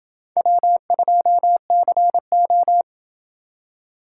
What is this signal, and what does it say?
Morse code, heard as a keyed tone, about 29 wpm, 700 Hz. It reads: W2CO